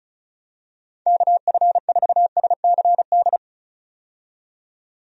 Morse code, heard as a keyed tone, about 34 wpm, 700 Hz. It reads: KF4SCD
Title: USA Call Signs - 6 Characters - 35wpm